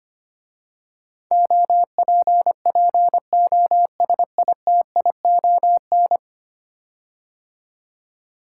Morse code, heard as a keyed tone, about 25 wpm, 700 Hz. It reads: OPPOSITION
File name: 10 Letter Words - First 3 Letters Given - 25wpm